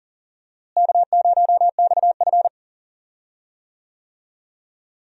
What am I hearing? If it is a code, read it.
Morse code, heard as a keyed tone, about 40 wpm, 700 Hz. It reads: K0XF
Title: USA Call Signs - 4 Characters - Repeated 2x - 40wpm